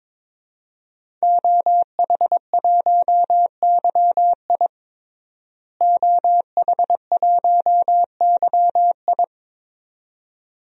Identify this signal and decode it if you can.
Morse code, heard as a keyed tone, about 22 wpm, 700 Hz. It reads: OH1YI OH1YI
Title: DXCC-100 Call Signs - 5 Characters - Repeated 2x - 22wpm